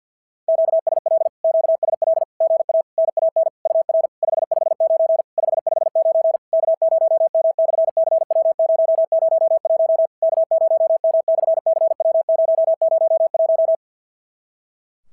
Morse code, heard as a keyed tone, about 53 wpm, 650 Hz. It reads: QSL QSL GA NAN UR 559 559 K0M/CW001 K0M/CW001
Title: SOTA QSO Practice 50wpm